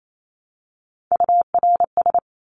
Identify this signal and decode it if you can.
Morse code, heard as a keyed tone, about 28 wpm, 700 Hz. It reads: URS
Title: Abbreviations Repeated 3x - 28wpm